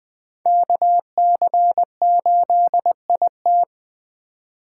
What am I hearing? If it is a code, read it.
Morse code, heard as a keyed tone, about 20 wpm, 700 Hz. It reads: KC8IT
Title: USA Call Signs - 4x Character Spacing - 20wpm